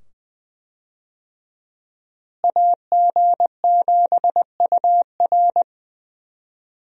Morse code, heard as a keyed tone, about 20 wpm, 700 Hz. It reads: AG7UR
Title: USA Call Signs - 5 Characters - Repeated 2x - 20wpm